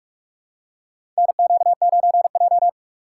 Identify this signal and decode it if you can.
Morse code, heard as a keyed tone, about 46 wpm, 700 Hz. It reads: NQ9J
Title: USA Call Signs - 4 Characters - 4x Character Spacing - 45wpm